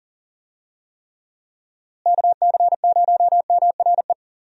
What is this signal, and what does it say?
Morse code, heard as a keyed tone, about 40 wpm, 700 Hz. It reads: KC0MRE
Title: USA Call Signs - 6 Characters - Repeated 2x - 40wpm